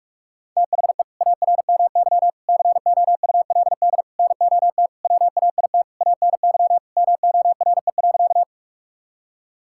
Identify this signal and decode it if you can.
Morse code, heard as a keyed tone, about 41 wpm, 690 Hz. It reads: THE ARMY COULD NOT WAIT ANY MORE.
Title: Sentences from Top 1000 Words - Review Part 5 - 45wpm